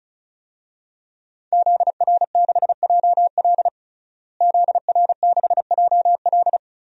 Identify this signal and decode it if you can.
Morse code, heard as a keyed tone, about 34 wpm, 700 Hz. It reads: ZR6JL ZR6JL
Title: DXCC-100 Call Signs - 5 Characters - Repeated 2x - 35wpm